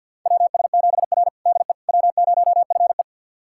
Morse code, heard as a keyed tone, about 49 wpm, 700 Hz. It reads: WS7R DE W0FE
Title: USA Callsign DE Callsign - 4 Characters - 50wpm